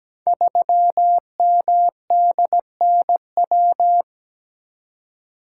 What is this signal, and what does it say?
Morse code, heard as a keyed tone, about 17 wpm, 700 Hz. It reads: 3MDNW
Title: Code Groups - 5 Letter:Numbers - 17wpm